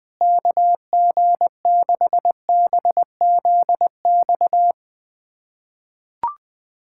Morse code, heard as a keyed tone, about 20 wpm, 700 Hz. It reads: KG6BZX
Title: USA Callsign DE Callsign - 6 Characters - 20wpm